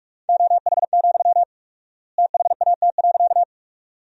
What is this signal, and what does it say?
Morse code, heard as a keyed tone, about 44 wpm, 700 Hz. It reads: OH, THAT.
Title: 4x Word-Spacing - Sentences from Top 500 Words - 45wpm